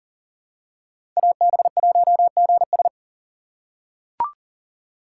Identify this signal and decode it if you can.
Morse code, heard as a keyed tone, about 40 wpm, 700 Hz. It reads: AB1GS
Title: USA Call Signs - 2x Character Spacing - 40wpm